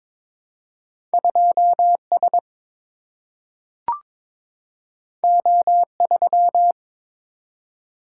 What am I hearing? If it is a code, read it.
Morse code, heard as a keyed tone, about 23 wpm, 700 Hz. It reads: 2S O3
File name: practice / 22wpm / Two Letter-Number Combinations - 22wpm